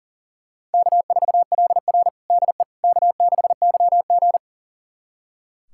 Morse code, heard as a keyed tone, about 38 wpm, 700 Hz. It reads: K4LR DE K6YG
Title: USA Callsign DE Callsign - 4 Characters - 40wpm